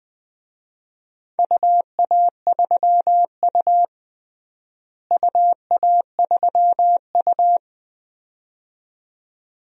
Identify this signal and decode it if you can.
Morse code, heard as a keyed tone, about 20 wpm, 700 Hz. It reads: UA3U UA3U
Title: DXCC-100 Call Signs - 4 Characters - Repeated 2x - 20wpm